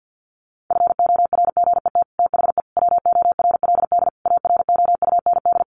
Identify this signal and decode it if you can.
Morse code, heard as a keyed tone, about 48 wpm, 700 Hz. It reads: FORGET THE WORLD AROUND
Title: Sentences from Top 1200 Words 50wpm